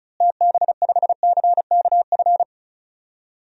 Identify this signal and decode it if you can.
Morse code, heard as a keyed tone, about 36 wpm, 700 Hz. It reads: TB5CKF
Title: DXCC-100 Call Signs - 6 Characters - Repeated 2x - 35wpm